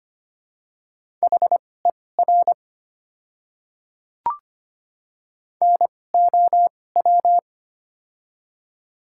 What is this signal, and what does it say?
Morse code, heard as a keyed tone, about 25 wpm, spaced as for 18 wpm, 700 Hz. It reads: HER NOW
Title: 2x Character Spacing - Top 300 Words - 25wpm